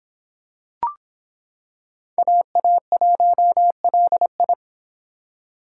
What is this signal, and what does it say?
Morse code, heard as a keyed tone, about 25 wpm, 700 Hz. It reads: AA1LI